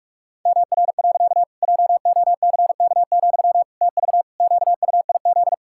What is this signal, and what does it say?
Morse code, heard as a keyed tone, about 41 wpm, 700 Hz. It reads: MR. JOCK, TV QUIZ
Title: Pangram Challenge v2 - 45wpm